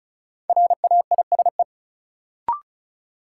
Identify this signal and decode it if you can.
Morse code, heard as a keyed tone, about 34 wpm, 700 Hz. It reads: RAISE
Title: Sentences from Top 1400 Words - New Word Drill - 35wpm